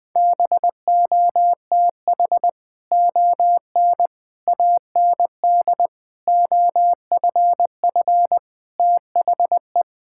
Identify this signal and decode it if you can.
Morse code, heard as a keyed tone, about 20 wpm, 700 Hz. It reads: BOTH ON AND OFF THE
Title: Sets of 6 Words - 20wpm